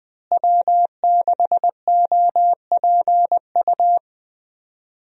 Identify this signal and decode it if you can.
Morse code, heard as a keyed tone, about 20 wpm, 700 Hz. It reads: W6OPU
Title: USA Call Signs - 5 Characters - Repeated 2x - 20wpm